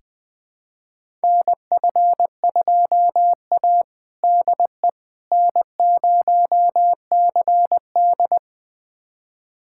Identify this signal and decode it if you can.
Morse code, heard as a keyed tone, about 20 wpm, 700 Hz. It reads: NF2A DE N0CD
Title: USA Callsign DE Callsign - 4 Characters - 20wpm